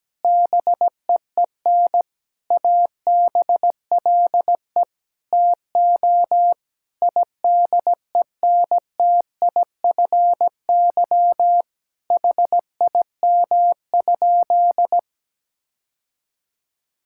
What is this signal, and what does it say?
Morse code, heard as a keyed tone, about 17 wpm, 700 Hz. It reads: BEEN ABLE TO IDENTIFY HIM?
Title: Sentences from Top 1000 Words - Review Part 6 - 17wpm